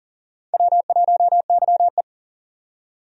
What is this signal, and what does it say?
Morse code, heard as a keyed tone, about 39 wpm, 700 Hz. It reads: W1YE
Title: USA Call Signs - 4 Characters - 40wpm